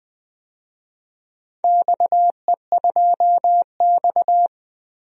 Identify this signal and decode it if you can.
Morse code, heard as a keyed tone, about 21 wpm, 700 Hz. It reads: XE2X
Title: DXCC-100 Call Signs - 4 Characters - Repeated 2x - 20wpm